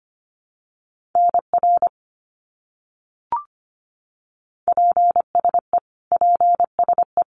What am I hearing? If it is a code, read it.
Morse code, heard as a keyed tone, about 25 wpm, 700 Hz. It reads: NR PSE PSE
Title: Abbreviations Repeated 3x - 25wpm